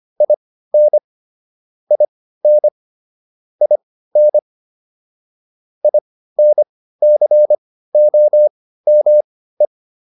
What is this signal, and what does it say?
Morse code, heard as a keyed tone, about 25 wpm, spaced as for 15 wpm, 600 Hz. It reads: IN IN IN INCOME